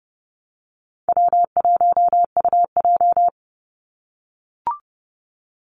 Morse code, heard as a keyed tone, about 30 wpm, 700 Hz. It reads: W1UJ
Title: Easy Call Signs - 30wpm